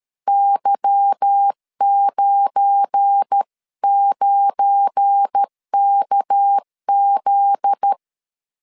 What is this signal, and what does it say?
Morse code, heard as a keyed tone, about 13 wpm, 800 Hz. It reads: Y99KZ
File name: CW / DET / 1 9 1500C S B 12wpm 3819